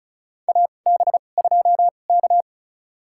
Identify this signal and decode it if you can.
Morse code, heard as a keyed tone, about 34 wpm, spaced as for 25 wpm, 700 Hz. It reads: AB2K
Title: USA Call Signs - 4 Characters - 2x Character Spacing - 35wpm